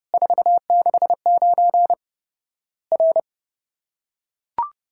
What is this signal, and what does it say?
Morse code, heard as a keyed tone, about 30 wpm, 700 Hz. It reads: 469 R
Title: Worked All Europe DX Contest - QTC Exchange - 30wpm